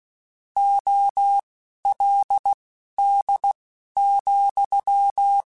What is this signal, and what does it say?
Morse code, heard as a keyed tone, about 16 wpm, spaced as for 12 wpm, 800 Hz. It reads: OLD,
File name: (spaced) / QSOs Spaced-mp3s / kc7sac @15 sp